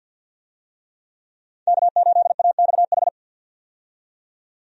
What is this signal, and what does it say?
Morse code, heard as a keyed tone, about 52 wpm, 700 Hz. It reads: K8AXH
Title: USA Call Signs - 5 Characters - 50wpm